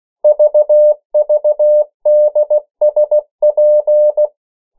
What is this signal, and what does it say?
Morse code, heard as a keyed tone, about 16 wpm, 600 Hz. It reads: VVDSP
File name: morse mp3 / 14 urok 13